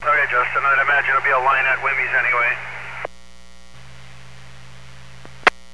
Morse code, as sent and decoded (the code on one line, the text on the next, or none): none